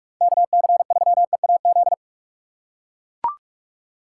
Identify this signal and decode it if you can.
Morse code, heard as a keyed tone, about 43 wpm, 700 Hz. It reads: KC3EAZ
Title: USA Call Signs - 6 Characters - 4x Character Spacing - 45wpm